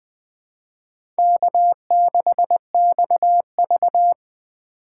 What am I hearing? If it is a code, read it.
Morse code, heard as a keyed tone, about 21 wpm, 700 Hz. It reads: K6XV
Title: USA Call Signs - 4 Characters - 2x Character Spacing - 20wpm